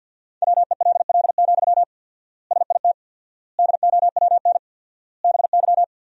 Morse code, heard as a keyed tone, about 52 wpm, 700 Hz. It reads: WELL, SIT DOWN BY
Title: 4x Word-Spacing - Sentences from Top 500 Words - 50wpm